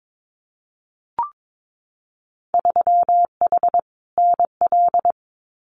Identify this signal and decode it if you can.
Morse code, heard as a keyed tone, about 22 wpm, 700 Hz. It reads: 3H NL
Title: Winter Field Day Exchanges Encore 22wpm